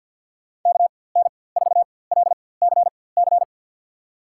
Morse code, heard as a keyed tone, about 53 wpm, spaced as for 24 wpm, 700 Hz. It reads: KN4LCC
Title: USA Call Signs - 6 Characters - 4x Character Spacing - 50wpm